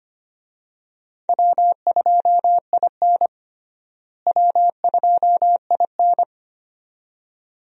Morse code, heard as a keyed tone, about 25 wpm, 700 Hz. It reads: W2IN W2IN
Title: USA Call Signs - 4 Characters - Repeated 2x - 25wpm